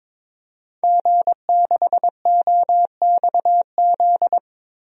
Morse code, heard as a keyed tone, about 22 wpm, 700 Hz. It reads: G6OXZ
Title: DXCC-100 Call Signs - 5 Characters - Repeated 3x - 22wpm